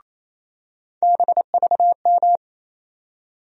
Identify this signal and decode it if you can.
Morse code, heard as a keyed tone, about 28 wpm, 700 Hz. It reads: BVM
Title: USA Call Sign Postfixes - 3 Letters 28wpm